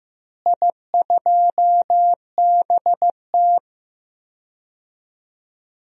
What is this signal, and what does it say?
Morse code, heard as a keyed tone, about 15 wpm, 700 Hz. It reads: I2BT
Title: DXCC-100 Call Signs - 4 Characters - Repeated 2x - 15wpm